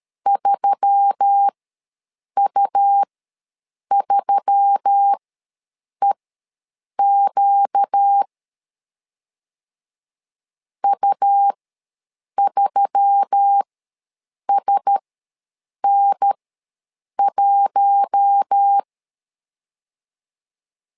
Morse code, heard as a keyed tone, about 13 wpm, spaced as for 7 wpm, 800 Hz. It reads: 3U3EQ U3SN1
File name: LEZIONI / 4 Q1S3 ws 12wpm 2038